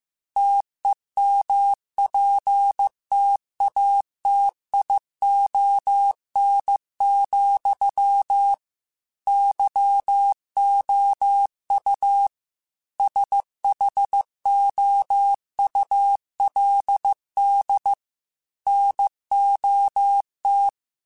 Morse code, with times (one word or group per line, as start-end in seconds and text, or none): 0.4-8.5 TEMPTATION,
9.3-12.3 YOU
13.0-17.9 SHOULD
18.7-20.7 NOT